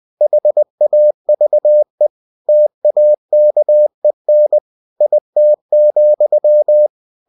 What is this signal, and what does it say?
Morse code, heard as a keyed tone, about 20 wpm, 600 Hz. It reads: HAVE TAKEN IT,
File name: cw-books / winnie-the-pooh / winnie-20wpm-page-0027